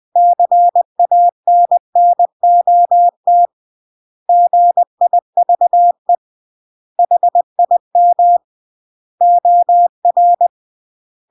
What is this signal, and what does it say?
Morse code, heard as a keyed tone, about 20 wpm, 700 Hz. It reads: CANNOT GIVE HIM OR
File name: CW / jodorowsky / 82 commandments 20-20